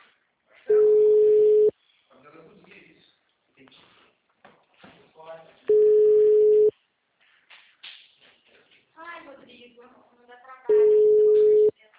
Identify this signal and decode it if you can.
Ringback tone (Europe, 425 Hz), with speech.